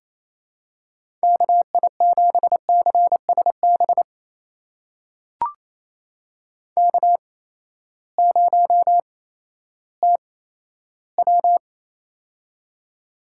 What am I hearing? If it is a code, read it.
Morse code, heard as a keyed tone, about 28 wpm, spaced as for 14 wpm, 700 Hz. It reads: KI7CSB K0TW
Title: USA Call Signs - 8x Character Spacing - 28wpm